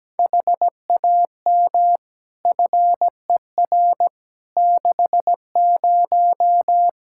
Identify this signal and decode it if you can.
Morse code, heard as a keyed tone, about 17 wpm, 700 Hz. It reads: HAM FER 60
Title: K1USN SST QSO Practice 17wpm